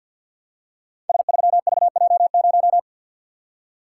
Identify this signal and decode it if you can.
Morse code, heard as a keyed tone, about 51 wpm, 700 Hz. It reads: S3VJ0
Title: Code Groups - 5 Letter:Numbers - 50wpm